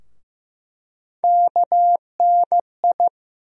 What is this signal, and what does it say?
Morse code, heard as a keyed tone, about 15 wpm, 700 Hz. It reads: KNI
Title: USA Call Sign Postfixes - 3 Letters 15wpm